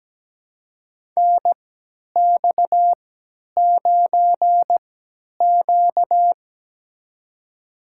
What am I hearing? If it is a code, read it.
Morse code, heard as a keyed tone, about 17 wpm, spaced as for 10 wpm, 700 Hz. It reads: NX9Q
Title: USA Call Signs - 3x Character Spacing - 17wpm